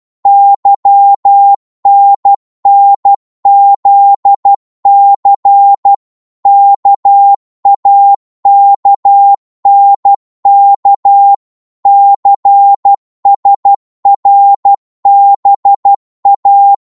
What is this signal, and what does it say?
Morse code, heard as a keyed tone, about 12 wpm, 800 Hz. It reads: YNNZC KAKNK CSRBA
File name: Morse / ltr-12wpm